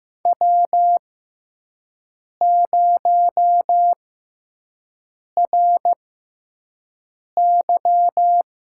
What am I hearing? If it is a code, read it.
Morse code, heard as a keyed tone, about 15 wpm, spaced as for 5 wpm, 700 Hz. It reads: W0RY
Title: USA Call Signs - 4 Characters - 6x Character Spacing - 15wpm